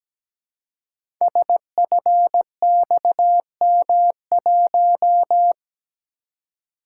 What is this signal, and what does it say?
Morse code, heard as a keyed tone, about 17 wpm, 700 Hz. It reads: SFXM1
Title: Code Groups - 5 Letter:Numbers - 17wpm